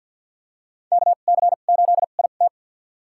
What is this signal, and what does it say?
Morse code, heard as a keyed tone, about 53 wpm, spaced as for 37 wpm, 700 Hz. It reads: KC8IT
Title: USA Call Signs - 5 Characters - 2x Character Spacing - 50wpm